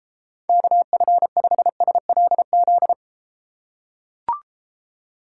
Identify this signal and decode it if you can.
Morse code, heard as a keyed tone, about 33 wpm, 700 Hz. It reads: KF5SLZ